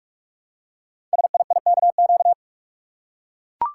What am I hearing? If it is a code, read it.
Morse code, heard as a keyed tone, about 44 wpm, 700 Hz. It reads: SIIKQ
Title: Code Groups - 5 Letters - 45wpm